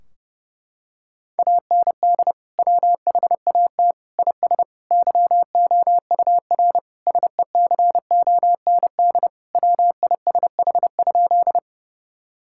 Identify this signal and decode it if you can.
Morse code, heard as a keyed tone, about 30 wpm, 700 Hz. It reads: AND WHAT IS YOUR SECOND WISH?
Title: Sentences from Top 1600 Words 30wpm